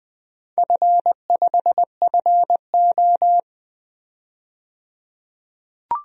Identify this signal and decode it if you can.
Morse code, heard as a keyed tone, about 20 wpm, 700 Hz. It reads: F5FO
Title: DXCC-100 Call Signs - 4 Characters - 20wpm